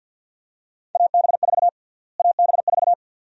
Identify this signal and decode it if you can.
Morse code, heard as a keyed tone, about 52 wpm, 700 Hz. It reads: AB4 AB4
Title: USA Call Sign Prefixes - Repeated 2x - 50wpm